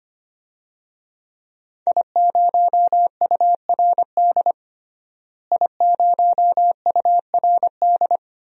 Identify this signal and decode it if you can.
Morse code, heard as a keyed tone, about 25 wpm, 700 Hz. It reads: I0URD I0URD
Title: DXCC-100 Call Signs - 5 Characters - Repeated 3x - 25wpm